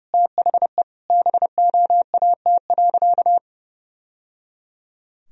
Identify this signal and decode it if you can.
Morse code, heard as a keyed tone, about 30 wpm, 700 Hz. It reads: THE BOAT.
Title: Sentences from Top 1000 Words - Review Part 4 - 30wpm